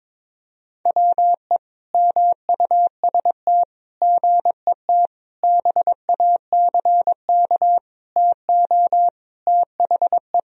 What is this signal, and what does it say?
Morse code, heard as a keyed tone, about 22 wpm, 700 Hz. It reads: WE MUST GET BACK TO THE
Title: Sentences from Top 1000 Words - Review Part 2 - 22wpm